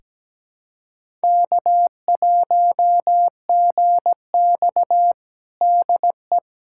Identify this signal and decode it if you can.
Morse code, heard as a keyed tone, about 17 wpm, 700 Hz. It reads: K1GX DE
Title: USA Callsign DE Callsign - 4 Characters - 17wpm